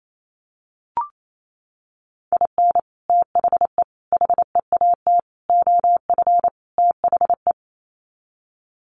Morse code, heard as a keyed tone, about 28 wpm, 700 Hz. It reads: IN THE HEAT OF THE
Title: Sets of 5 Words Encore2 28wpm